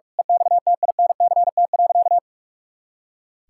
Morse code, heard as a keyed tone, about 46 wpm, 700 Hz. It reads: EXTINCT.
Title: Pangram Challenge v2 - 45wpm